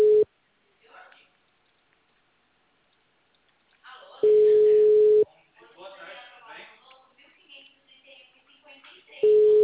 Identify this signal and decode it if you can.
Ringback tone (Europe, 425 Hz), with speech.
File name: out- -2006- - - .500